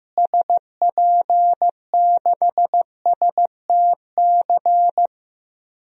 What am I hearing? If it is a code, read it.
Morse code, heard as a keyed tone, about 15 wpm, 700 Hz. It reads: SP6STC
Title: DXCC-100 Call Signs - 6 Characters - Repeated 3x - 15wpm